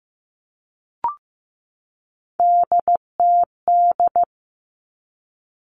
Morse code, heard as a keyed tone, about 15 wpm, 700 Hz. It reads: DTD
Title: Three Challenging Letter-Number Combinations 15wpm